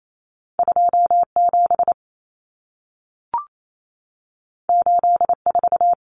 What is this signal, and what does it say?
Morse code, heard as a keyed tone, about 28 wpm, 700 Hz. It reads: 27 84
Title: Numbers - 2-Digit 28wpm